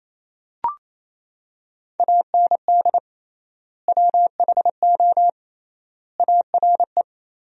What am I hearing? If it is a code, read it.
Morse code, heard as a keyed tone, about 28 wpm, 700 Hz. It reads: AND WHO ARE
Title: 3x Word-Spacing - Sentences from Top 100 Words - 28wpm